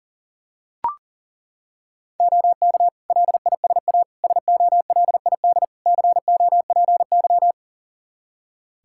Morse code, heard as a keyed tone, about 37 wpm, 700 Hz. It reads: OK LISA SOLID COPY
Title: Practice QSO - Exchange 3 - More Info - 40wpm